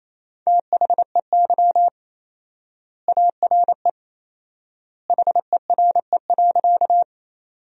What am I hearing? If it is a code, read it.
Morse code, heard as a keyed tone, about 28 wpm, 700 Hz. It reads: THEY ARE HERE.
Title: 4x Word-Spacing - Sentences from Top 200 Words - 28wpm